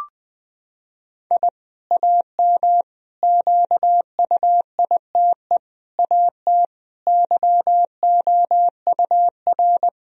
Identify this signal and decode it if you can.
Morse code, heard as a keyed tone, about 20 wpm, 700 Hz. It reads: I AM QUITE AT YOUR
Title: Sentences from Top 1000 Words - Review Part 4 - 20wpm